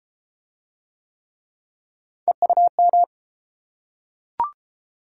Morse code, heard as a keyed tone, about 30 wpm, 700 Hz. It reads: EUM